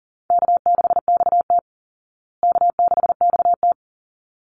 Morse code, heard as a keyed tone, about 40 wpm, 700 Hz. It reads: K6XT K6XT
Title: Easy Call Signs Repeated 2x - 40wpm